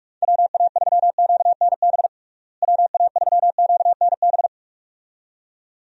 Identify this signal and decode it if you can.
Morse code, heard as a keyed tone, about 41 wpm, 700 Hz. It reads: WA3QNB WA3QNB
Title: USA Call Signs - 6 Characters - Repeated 2x - 45wpm